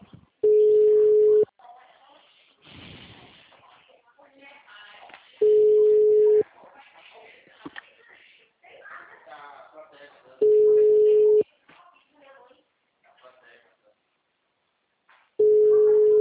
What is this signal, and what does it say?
Ringback tone (Europe, 425 Hz), with speech.